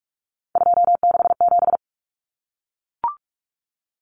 Morse code, heard as a keyed tone, about 44 wpm, 700 Hz. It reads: 267
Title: Numbers - 3-Digit 45wpm